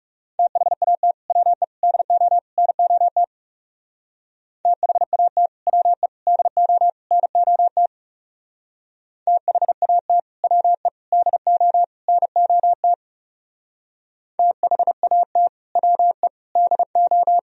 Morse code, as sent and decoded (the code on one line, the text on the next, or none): - .... .- - / .-- . / -.. --- / -. --- - / - .... .- - / .-- . / -.. --- / -. --- - / - .... .- - / .-- . / -.. --- / -. --- - / - .... .- - / .-- . / -.. ---
THAT WE DO NOT THAT WE DO NOT THAT WE DO NOT THAT WE DO